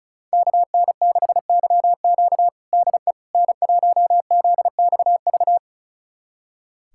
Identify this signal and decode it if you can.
Morse code, heard as a keyed tone, about 36 wpm, 700 Hz. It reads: KN6YQ DE N1ZXV
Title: USA Callsign DE Callsign - 5 Characters - 35wpm